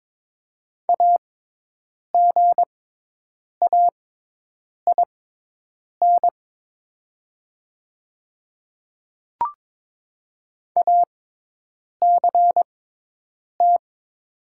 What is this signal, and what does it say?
Morse code, heard as a keyed tone, about 22 wpm, spaced as for 8 wpm, 700 Hz. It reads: AGAIN ACT
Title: 6x Character Spacing - Top 200 Words - 22wpm